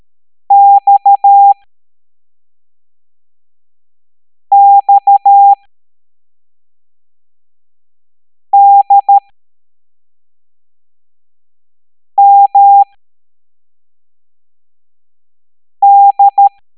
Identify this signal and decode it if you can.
Morse code, heard as a keyed tone, about 13 wpm, spaced as for 3 wpm, 800 Hz. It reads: XXDMD
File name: hamdata / L02-EX01-MDX